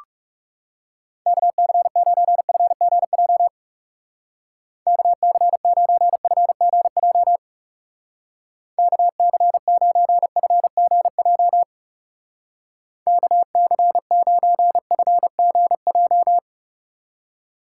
Morse code, heard as a keyed tone, about 40 wpm, 700 Hz. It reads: KC9FGJ KC9FGJ KC9FGJ KC9FGJ